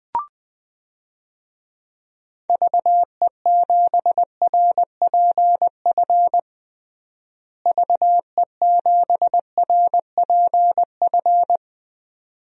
Morse code, heard as a keyed tone, about 20 wpm, 700 Hz. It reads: VE7RPF VE7RPF
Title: DXCC-100 Call Signs - 6 Characters - Repeated 2x - 20wpm